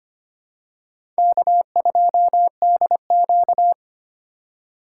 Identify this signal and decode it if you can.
Morse code, heard as a keyed tone, about 25 wpm, 700 Hz. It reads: K2DQ
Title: USA Call Signs - 4 Characters - 3x Character Spacing - 25wpm